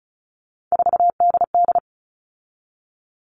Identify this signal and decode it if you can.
Morse code, heard as a keyed tone, about 34 wpm, 700 Hz. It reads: 4DD
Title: All Cut Numbers - 3-Digit 35wpm